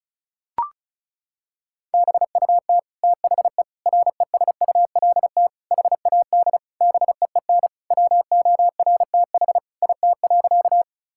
Morse code, heard as a keyed tone, about 36 wpm, 700 Hz. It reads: BUT THE RESULT HAD BEEN WORTH IT.
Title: Sentences from Top 1000 Words - Review Part 5 - 35wpm